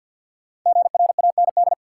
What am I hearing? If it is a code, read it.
Morse code, heard as a keyed tone, about 53 wpm, 700 Hz. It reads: GRAND
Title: Five Letter Words - 50wpm